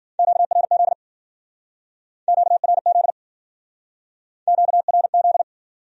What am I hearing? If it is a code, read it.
Morse code, heard as a keyed tone, about 51 wpm, 700 Hz. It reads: QRZ QRZ QRZ